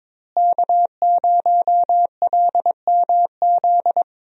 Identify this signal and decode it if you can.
Morse code, heard as a keyed tone, about 21 wpm, 700 Hz. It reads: K0LMZ
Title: USA Call Signs - 5 Characters - Repeated 3x - 22wpm